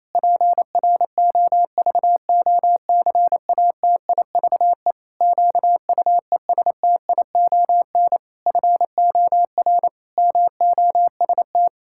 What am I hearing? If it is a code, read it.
Morse code, heard as a keyed tone, about 28 wpm, 700 Hz. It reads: PROVOCATIVE QUESTION FOR MOST
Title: Pangram Challenge v2 - 28wpm